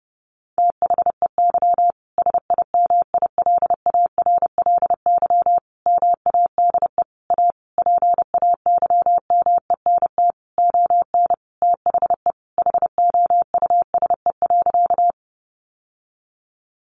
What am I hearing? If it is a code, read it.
Morse code, heard as a keyed tone, about 30 wpm, 700 Hz. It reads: THEY SIMILARLY MADE A PAYMENT ON THE HOUSE.
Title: Sentences from Top 1400 Words 30wpm